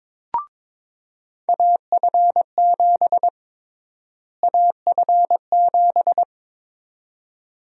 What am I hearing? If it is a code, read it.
Morse code, heard as a keyed tone, about 22 wpm, 700 Hz. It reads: AF7 AF7
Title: USA Call Sign Prefixes - Repeated 2x - 22wpm